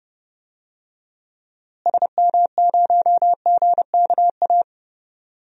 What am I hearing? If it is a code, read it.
Morse code, heard as a keyed tone, about 32 wpm, 700 Hz. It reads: SM0GKA
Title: DXCC-100 Call Signs - 6 Characters - Repeated 3x - 30wpm